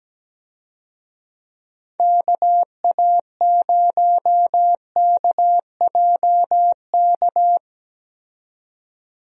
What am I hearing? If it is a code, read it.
Morse code, heard as a keyed tone, about 17 wpm, 700 Hz. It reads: KA0KJK
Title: USA Call Signs - 6 Characters - 17wpm